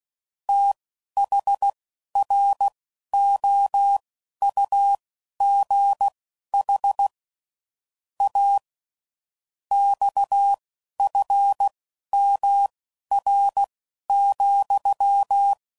Morse code, heard as a keyed tone, about 16 wpm, spaced as for 11 wpm, 800 Hz. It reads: THROUGH A XFMR,
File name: CW Code Groups @10WPM / Technical Subjects / SwPwrSup